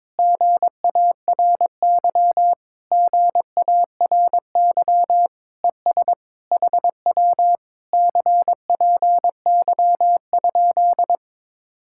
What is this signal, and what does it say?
Morse code, heard as a keyed tone, about 22 wpm, 700 Hz. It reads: GARY GARY ES HW CPY?
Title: Speed-Racing Common QSO Phrases - 22wpm